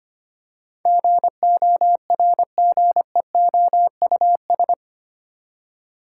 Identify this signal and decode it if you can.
Morse code, heard as a keyed tone, about 25 wpm, 700 Hz. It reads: GORGEOUS
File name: Eight Letter Words - 25wpm